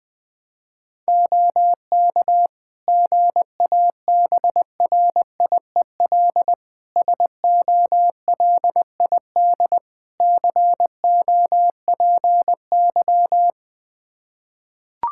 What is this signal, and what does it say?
Morse code, heard as a keyed tone, about 20 wpm, 700 Hz. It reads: OK GABRIEL SOLID COPY
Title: Practice QSO - Exchange 3 - More Info - 20wpm